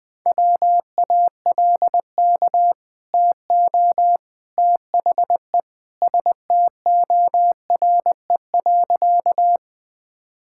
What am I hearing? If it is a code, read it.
Morse code, heard as a keyed tone, about 20 wpm, 700 Hz. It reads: WALK TO THE STORE.
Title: Easy Sentences with Hints - 20wpm